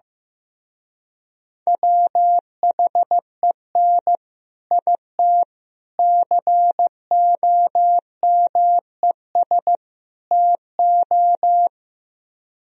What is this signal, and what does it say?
Morse code, heard as a keyed tone, about 15 wpm, 700 Hz. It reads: WHEN IT COMES TO
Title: Speed-Racing - Sets of 4 Words 15wpm